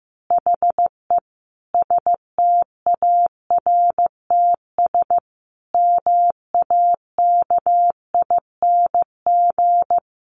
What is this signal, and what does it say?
Morse code, heard as a keyed tone, about 15 wpm, 700 Hz. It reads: HE STARTS MAKING
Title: Additional Easy Phrases 15wpm